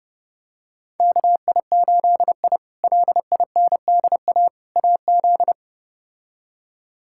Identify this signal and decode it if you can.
Morse code, heard as a keyed tone, about 30 wpm, 700 Hz. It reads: KI8I LINDA AZ
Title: K1USN SST QSO Practice - 30wpm